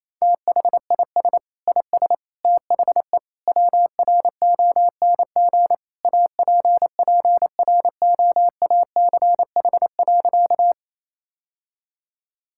Morse code, heard as a keyed tone, about 28 wpm, 700 Hz. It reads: THIS IS THE WRONG APPROACH.
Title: Sentences from Top 1000 Words - Review Part 6 - 28wpm